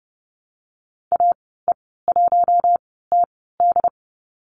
Morse code, heard as a keyed tone, about 30 wpm, spaced as for 17 wpm, 700 Hz. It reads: AE1TD
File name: Lesson 14g - Number 2 - Call Signs 30wpm